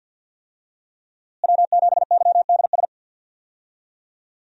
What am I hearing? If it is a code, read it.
Morse code, heard as a keyed tone, about 52 wpm, 700 Hz. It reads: W7YDS
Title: USA Call Signs - 5 Characters - Repeated 3x - 50wpm